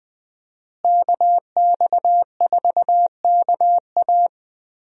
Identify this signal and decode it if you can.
Morse code, heard as a keyed tone, about 20 wpm, 700 Hz. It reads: KX4KA
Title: USA Call Signs - 5 Characters - Repeated 2x - 20wpm